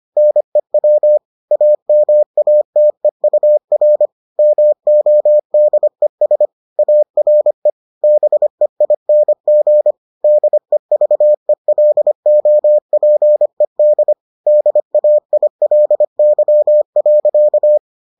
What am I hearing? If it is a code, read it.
Morse code, heard as a keyed tone, about 25 wpm, 600 Hz. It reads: NEW AMATEUR MODES ARE BEING DEVELOPED DAILY.